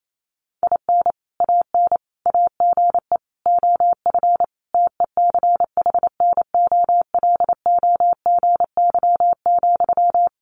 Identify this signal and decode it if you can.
Morse code, heard as a keyed tone, about 28 wpm, 700 Hz. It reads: IN AN AGE OF TECHNOLOGY,
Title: Sentences from Top 1200 Words 28wpm